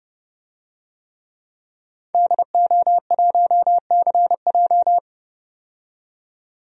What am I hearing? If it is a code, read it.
Morse code, heard as a keyed tone, about 30 wpm, 700 Hz. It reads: DO1CJ
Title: DXCC-100 Call Signs - 5 Characters - 30wpm